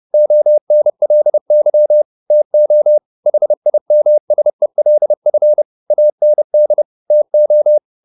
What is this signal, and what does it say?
Morse code, heard as a keyed tone, about 30 wpm, 600 Hz. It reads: ONLY TO HIMSELF AND TO